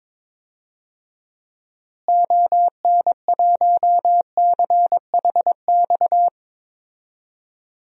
Morse code, heard as a keyed tone, about 22 wpm, 700 Hz. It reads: ON1CHX
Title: DXCC-100 Call Signs - 6 Characters - Repeated 2x - 22wpm